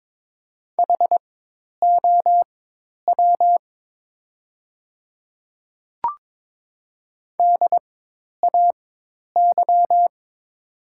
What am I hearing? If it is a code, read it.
Morse code, heard as a keyed tone, about 22 wpm, spaced as for 10 wpm, 700 Hz. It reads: HOW DAY
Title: 4x Character Spacing - Top 100 Words - 22wpm